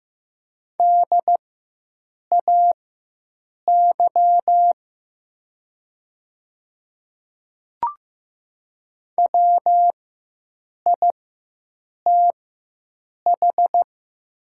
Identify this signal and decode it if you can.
Morse code, heard as a keyed tone, about 15 wpm, spaced as for 7 wpm, 700 Hz. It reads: DAY WITH
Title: 4x Character Spacing - Top 100 Words - 15wpm